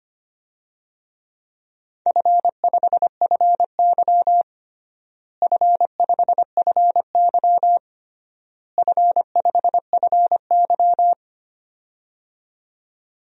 Morse code, heard as a keyed tone, about 25 wpm, 700 Hz. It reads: F5FY F5FY F5FY
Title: DXCC-100 Call Signs - 4 Characters - Repeated 3x - 25wpm